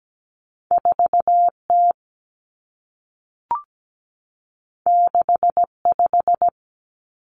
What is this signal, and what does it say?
Morse code, heard as a keyed tone, about 17 wpm, 700 Hz. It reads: 4T 65
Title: Common Cut Numbers - 2-Digit 17wpm